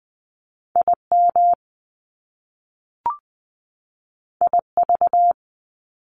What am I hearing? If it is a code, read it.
Morse code, heard as a keyed tone, about 20 wpm, 700 Hz. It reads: IM IV
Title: Top Two Letter Combinations in Words 20wpm